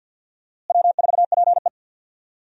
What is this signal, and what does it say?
Morse code, heard as a keyed tone, about 52 wpm, 700 Hz. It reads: W4PE
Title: USA Call Signs - 4 Characters - Repeated 3x - 50wpm